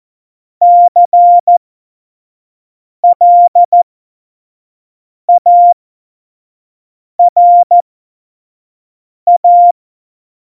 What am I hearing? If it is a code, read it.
Morse code, heard as a keyed tone, about 14 wpm, spaced as for 5 wpm, 700 Hz. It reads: CLARA
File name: Morsekurs / Witze / Witz 01. Dezember, 2020 25 F70